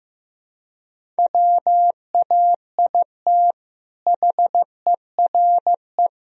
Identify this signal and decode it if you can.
Morse code, heard as a keyed tone, about 16 wpm, 700 Hz. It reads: WAIT HERE
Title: Sentences from Top 1000 Words - Review Part 4 - 15wpm